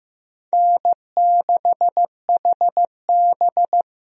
Morse code, heard as a keyed tone, about 15 wpm, 700 Hz. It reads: N6HB
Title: USA Call Signs - 4 Characters - Repeated 2x - 15wpm